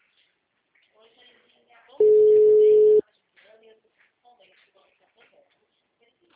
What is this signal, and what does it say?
Ringback tone (Europe, 425 Hz), with speech.